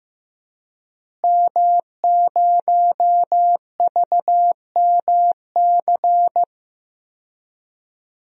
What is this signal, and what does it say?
Morse code, heard as a keyed tone, about 15 wpm, 700 Hz. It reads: M0VMC
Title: DXCC-100 Call Signs - 5 Characters - Repeated 2x - 15wpm